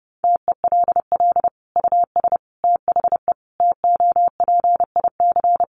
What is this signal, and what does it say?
Morse code, heard as a keyed tone, about 30 wpm, 700 Hz. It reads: TELL US THE TOPIC
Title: Sentences from Top 1200 Words 30wpm